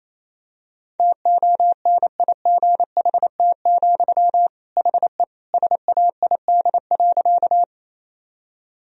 Morse code, heard as a keyed tone, about 28 wpm, 700 Hz. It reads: TONIGHT, HE SAID.
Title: Sentences from Top 1000 Words - Review Part 3 - 28wpm